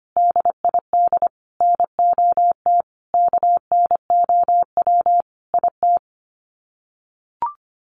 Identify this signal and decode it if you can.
Morse code, heard as a keyed tone, about 25 wpm, 700 Hz. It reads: DID NOT KNOW IT
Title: Sets of 5 Words Encore2 25wpm